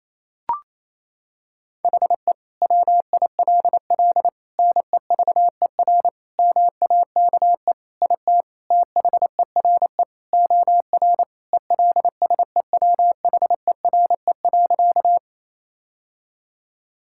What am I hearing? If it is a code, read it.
Morse code, heard as a keyed tone, about 28 wpm, 700 Hz. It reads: HE WILL NEVER MAKE IT THERE OR ELSEWHERE.
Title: Sentences from Top 1600 Words 28wpm